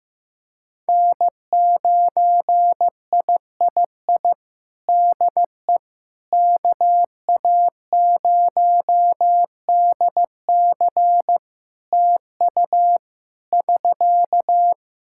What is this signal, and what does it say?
Morse code, heard as a keyed tone, about 15 wpm, 700 Hz. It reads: N9III DE KA0DC TU <SK>
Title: Practice QSO - Exchange 4 - Ending - 15wpm